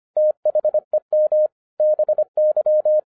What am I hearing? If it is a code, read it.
Morse code, heard as a keyed tone, about 25 wpm, 610 Hz. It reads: THEM BY
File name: 2011-07-mp3 / n24 QRP OTAKU 25wpm